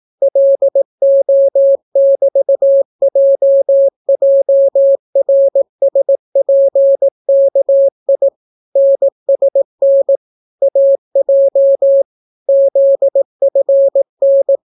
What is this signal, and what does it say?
Morse code, heard as a keyed tone, about 18 wpm, 550 Hz. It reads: LO<BT>JJRSPKI NSN AJ ZFN